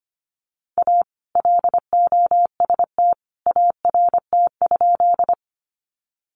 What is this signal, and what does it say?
Morse code, heard as a keyed tone, about 25 wpm, 700 Hz. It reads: A LOST ART?
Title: Sentences from Top 1400 Words 25wpm